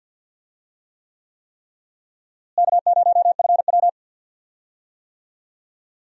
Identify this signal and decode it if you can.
Morse code, heard as a keyed tone, about 53 wpm, 700 Hz. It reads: K0FW
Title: USA Call Signs - 4 Characters - Repeated 3x - 50wpm